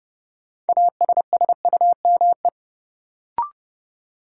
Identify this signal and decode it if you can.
Morse code, heard as a keyed tone, about 30 wpm, 700 Hz. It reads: ASSUME
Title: Sentences from Top 1500 Words - New Word Drill Alt - 30wpm